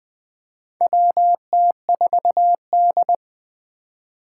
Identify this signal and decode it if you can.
Morse code, heard as a keyed tone, about 20 wpm, 700 Hz. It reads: WT4D
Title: USA Call Signs - 4 Characters - 4x Character Spacing - 20wpm